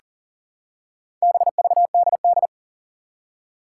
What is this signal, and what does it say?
Morse code, heard as a keyed tone, about 37 wpm, 700 Hz. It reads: BVDD
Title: Four Challenging Letter-Number Combinations 40wpm